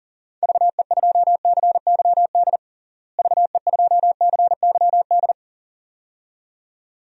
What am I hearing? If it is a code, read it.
Morse code, heard as a keyed tone, about 40 wpm, 700 Hz. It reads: VE2CYD VE2CYD
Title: DXCC-100 Call Signs - 6 Characters - Repeated 2x - 40wpm